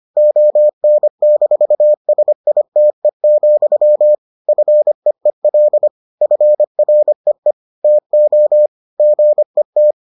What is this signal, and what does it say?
Morse code, heard as a keyed tone, about 25 wpm, 600 Hz. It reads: ON-SITE, FEEL FREE TO GET